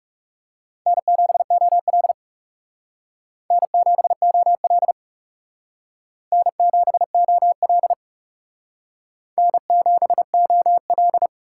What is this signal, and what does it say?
Morse code, heard as a keyed tone, about 38 wpm, 700 Hz. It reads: N7OL N7OL N7OL N7OL